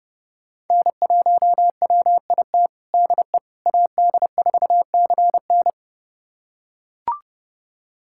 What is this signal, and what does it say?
Morse code, heard as a keyed tone, about 32 wpm, 700 Hz. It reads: N1WIT DE AD4CN
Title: USA Callsign DE Callsign - 5 Characters - 30wpm